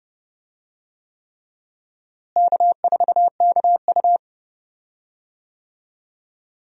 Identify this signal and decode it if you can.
Morse code, heard as a keyed tone, about 30 wpm, 700 Hz. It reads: K4KU
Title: USA Call Signs - 4 Characters - Repeated 3x - 30wpm